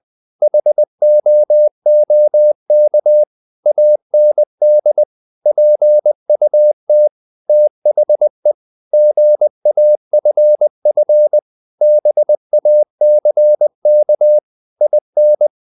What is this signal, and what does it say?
Morse code, heard as a keyed tone, about 20 wpm, 600 Hz. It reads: HOOK AND PUT THE GAFF BACK IN
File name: the old man and the sea / the old man and the sea part 1 20wpm